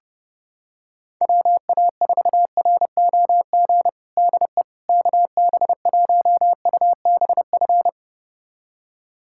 Morse code, heard as a keyed tone, about 30 wpm, 700 Hz. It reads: WA4ROG DE KB1UBF
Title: USA Callsign DE Callsign - 6 Characters - 30wpm